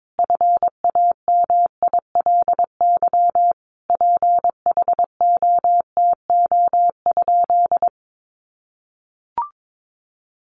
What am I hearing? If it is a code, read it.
Morse code, heard as a keyed tone, about 22 wpm, 700 Hz. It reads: FAMILY PHOTO?
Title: Sentences from Top 1200 Words 22wpm